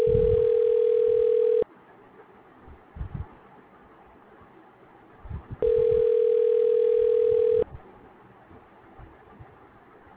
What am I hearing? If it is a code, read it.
Ringback tone (North America), 440 + 480 Hz.